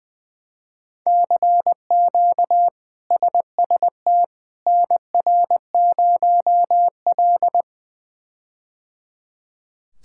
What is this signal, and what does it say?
Morse code, heard as a keyed tone, about 20 wpm, 700 Hz. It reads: CQ SST NR0L
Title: K1USN SST QSO Practice - 20wpm